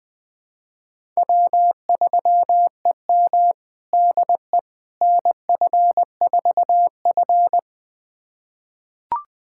Morse code, heard as a keyed tone, about 21 wpm, 700 Hz. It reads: W3EM DE NF4F
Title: USA Callsign DE Callsign - 4 Characters - 20wpm